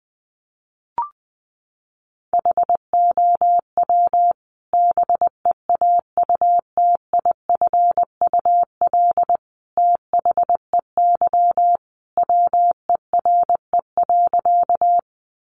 Morse code, heard as a keyed tone, about 19 wpm, 700 Hz. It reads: HOW BEAUTIFUL THEY WERE.
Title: Sentences from Top 900 Words 20wpm